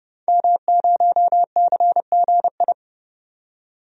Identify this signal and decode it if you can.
Morse code, heard as a keyed tone, about 30 wpm, 700 Hz. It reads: M0CGI
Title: DXCC-100 Call Signs - 5 Characters - 30wpm